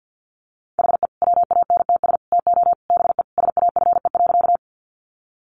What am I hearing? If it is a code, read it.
Morse code, heard as a keyed tone, about 50 wpm, 700 Hz. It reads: HE WANTS TO BE SAFE.
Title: Additional Easy Phrases 50wpm